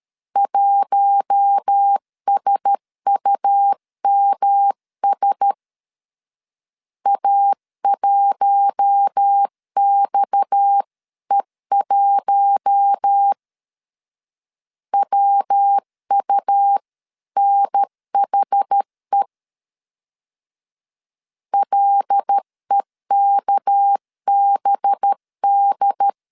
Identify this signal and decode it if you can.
Morse code, heard as a keyed tone, about 13 wpm, 800 Hz. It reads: 1SUMS A1XE1 WUNHE LEKBD
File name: CW / DET / 1 6 1500C S C 12wpm 3533